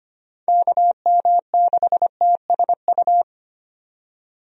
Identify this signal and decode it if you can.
Morse code, heard as a keyed tone, about 25 wpm, 700 Hz. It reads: KM6TSU
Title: USA Call Signs - 6 Characters - 25wpm